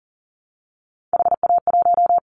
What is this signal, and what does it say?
Morse code, heard as a keyed tone, about 40 wpm, 700 Hz. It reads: HA1
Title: Callsign Prefixes for Top 25 DXCC - Repeated 3x - 40wpm